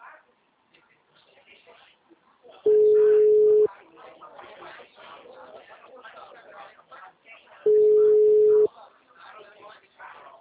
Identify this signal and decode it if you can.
Ringback tone (Europe, 425 Hz), with speech.